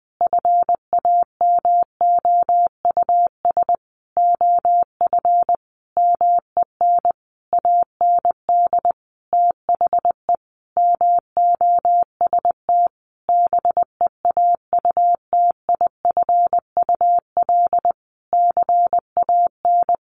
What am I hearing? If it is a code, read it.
Morse code, heard as a keyed tone, about 20 wpm, 700 Hz. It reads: FAMOUS OF MEN AND THE MOST BEAUTIFUL CAN
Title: Sentences from Top 1500 Words 20wpm